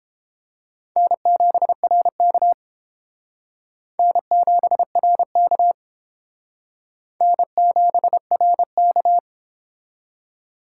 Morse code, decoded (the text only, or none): N7RK N7RK N7RK